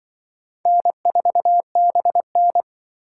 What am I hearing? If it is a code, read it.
Morse code, heard as a keyed tone, about 25 wpm, 700 Hz. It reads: N4BN
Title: Speed Racing - USA 1x2 Call Signs 20wpm